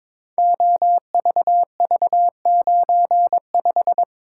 Morse code, heard as a keyed tone, about 22 wpm, 700 Hz. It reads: OVV95
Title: Code Groups - 5 Letter:Numbers - 22wpm